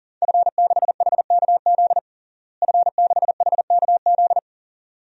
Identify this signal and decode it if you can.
Morse code, heard as a keyed tone, about 40 wpm, 700 Hz. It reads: F6HKZ F6HKZ
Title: DXCC-100 Call Signs - 5 Characters - Repeated 2x - 40wpm